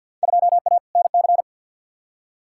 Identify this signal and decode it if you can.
Morse code, heard as a keyed tone, about 52 wpm, 700 Hz. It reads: 2A NC
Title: Field Day Exchanges - 50wpm